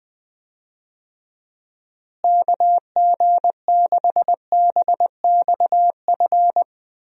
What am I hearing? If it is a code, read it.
Morse code, heard as a keyed tone, about 20 wpm, 700 Hz. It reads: KG6BXF
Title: USA Call Signs - 6 Characters - Repeated 3x - 20wpm